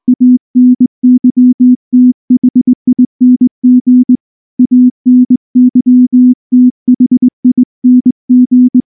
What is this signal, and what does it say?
Morse code, heard as a keyed tone, about 19 wpm, 250 Hz. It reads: ANYTHING ANYTHING
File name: morse / morse-training-20WPM-3